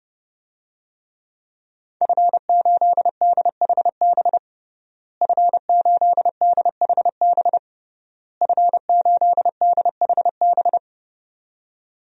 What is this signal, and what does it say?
Morse code, heard as a keyed tone, about 30 wpm, 700 Hz. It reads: F8DHB F8DHB F8DHB
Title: DXCC-100 Call Signs - 5 Characters - Repeated 3x - 30wpm